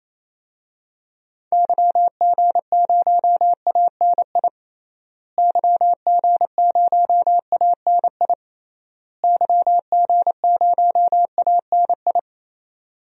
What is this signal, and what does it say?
Morse code, heard as a keyed tone, about 27 wpm, 700 Hz. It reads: YG0ANI YG0ANI YG0ANI
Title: DXCC-100 Call Signs - 6 Characters - Repeated 3x - 28wpm